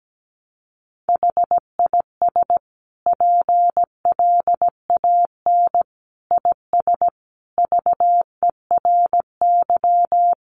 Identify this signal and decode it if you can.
Morse code, heard as a keyed tone, about 17 wpm, 700 Hz. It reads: HIS PLAN IS VERY
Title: Sentences from Top 500 Words 17wpm